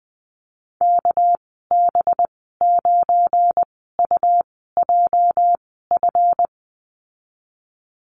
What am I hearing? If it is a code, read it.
Morse code, heard as a keyed tone, about 20 wpm, spaced as for 14 wpm, 700 Hz. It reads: KB9UJF
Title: 207 Lesson 33g - Letter j - Call Signs 20wpm